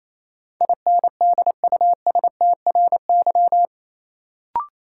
Morse code, heard as a keyed tone, about 28 wpm, 700 Hz. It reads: INDUSTRY
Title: Sentences from Top 1000 Words - Review Part 4 - New Word Drill - 28wpm